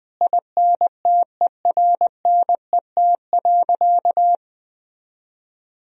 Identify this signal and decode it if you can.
Morse code, heard as a keyed tone, about 20 wpm, 700 Hz. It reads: INTERNET.
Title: Sentences from Top 1000 Words - Review Part 4 - 20wpm